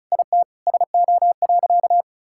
Morse code, heard as a keyed tone, about 34 wpm, 700 Hz. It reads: IT SO.
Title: Sentences from Top 1600 Words 35wpm